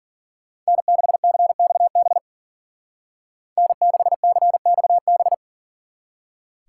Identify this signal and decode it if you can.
Morse code, heard as a keyed tone, about 48 wpm, 700 Hz. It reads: N6CXB N6CXB
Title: Speed Racing - USA 1x3 Call Signs 40wpm